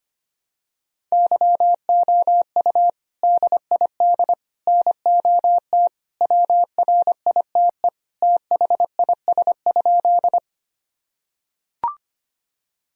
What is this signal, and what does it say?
Morse code, heard as a keyed tone, about 25 wpm, 700 Hz. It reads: YOU DID NOT WRITE THIS?
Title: Sentences from Top 1000 Words - Review Part 1 - 25wpm